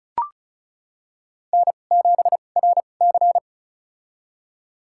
Morse code, heard as a keyed tone, about 36 wpm, spaced as for 25 wpm, 700 Hz. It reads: N7RC
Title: USA Call Signs - 4 Characters - 2x Character Spacing - 35wpm